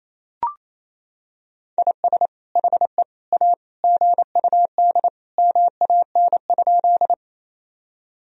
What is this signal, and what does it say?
Morse code, heard as a keyed tone, about 28 wpm, 700 Hz. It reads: IS HE A GUD MAN?
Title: Sentences from Top 200 Words and Abbreviations 28wpm